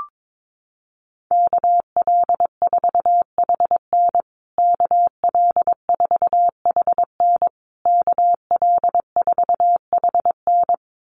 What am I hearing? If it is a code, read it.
Morse code, heard as a keyed tone, about 22 wpm, 700 Hz. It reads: KL4HN KL4HN KL4HN
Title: USA Call Signs Repeated 3x 22wpm